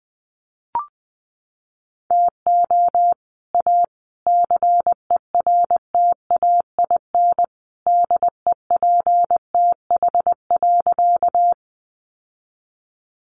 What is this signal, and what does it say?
Morse code, heard as a keyed tone, about 20 wpm, 700 Hz. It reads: TO A CERTAIN DEPTH.
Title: Sentences from Top 1200 Words 20wpm